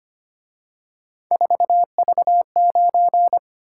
Morse code, heard as a keyed tone, about 29 wpm, 700 Hz. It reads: 4V9
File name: Speed-Racing - Three Letter-Number Combinations - 25wpm